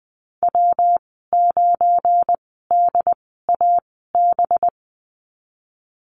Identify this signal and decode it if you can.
Morse code, heard as a keyed tone, about 19 wpm, spaced as for 14 wpm, 700 Hz. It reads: W9DAB
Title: 193 Lesson 31g - Letter v - Call Signs 20wpm